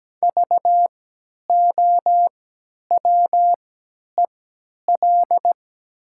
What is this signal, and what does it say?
Morse code, heard as a keyed tone, about 17 wpm, spaced as for 10 wpm, 700 Hz. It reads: VOWEL
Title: 3x Character Spacing - Top 500 Words - 17wpm